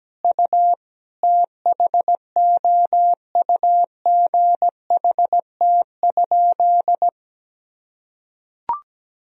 Morse code, heard as a keyed tone, about 17 wpm, 700 Hz. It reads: U THOUGHT?
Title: Sentences from Top 300 Words and Abbreviations 17wpm